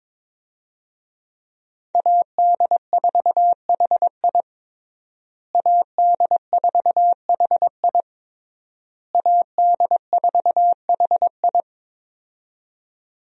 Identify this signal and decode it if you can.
Morse code, heard as a keyed tone, about 22 wpm, 700 Hz. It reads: AD4HI AD4HI AD4HI
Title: USA Call Signs - 5 Characters - Repeated 3x - 22wpm